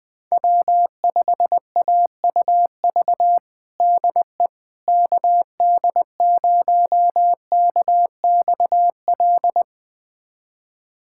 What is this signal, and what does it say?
Morse code, heard as a keyed tone, about 20 wpm, 700 Hz. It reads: W5AUV DE KD0KXL
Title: USA Callsign DE Callsign 20wpm